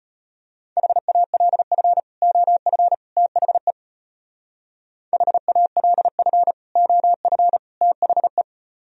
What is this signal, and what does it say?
Morse code, heard as a keyed tone, about 30 wpm, 700 Hz. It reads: HALF OF THE HALF OF THE
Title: Speed-Racing - Sets of 3 Words 25wpm